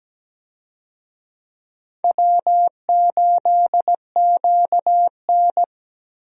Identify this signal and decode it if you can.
Morse code, heard as a keyed tone, about 17 wpm, 700 Hz. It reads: W8QN
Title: USA Call Signs - 4 Characters - Repeated 3x - 17wpm